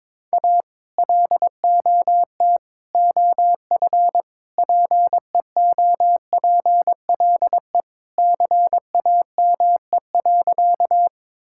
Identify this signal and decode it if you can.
Morse code, heard as a keyed tone, about 22 wpm, 700 Hz. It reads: A LOT OF PEOPLE CAME.
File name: practice / 22wpm / Sentences with 3-Word Phrases - 22wpm